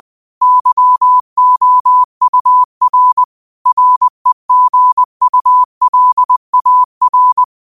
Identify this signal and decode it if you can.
Morse code, heard as a keyed tone, about 20 wpm, 1000 Hz. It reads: YOUR REGULAR
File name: qotdfiles / 20wpm